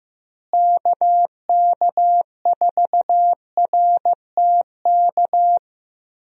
Morse code, heard as a keyed tone, about 15 wpm, 700 Hz. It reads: KK4RTK
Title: USA Call Signs - 6 Characters - Repeated 2x - 15wpm